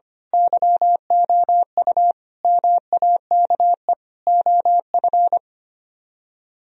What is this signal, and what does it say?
Morse code, heard as a keyed tone, about 25 wpm, 700 Hz. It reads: YOU MAKE OF
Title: Speed-Racing - Sets of 5 Words 25wpm